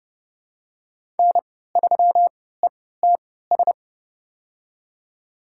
Morse code, heard as a keyed tone, about 30 wpm, spaced as for 17 wpm, 700 Hz. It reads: N3ETS
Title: USA Call Signs - 5 Characters - 3x Character Spacing - 30wpm